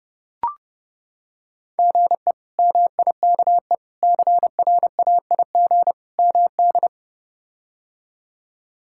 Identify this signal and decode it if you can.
Morse code, heard as a keyed tone, about 30 wpm, 700 Hz. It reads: GE MIKE CRAIG MD
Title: K1USN SST QSO Practice - 30wpm